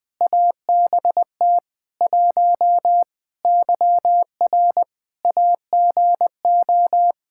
Morse code, heard as a keyed tone, about 21 wpm, 700 Hz. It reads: ABT 1 YR AGO
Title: Practice QSO - Exchange 3 - More Info - 20wpm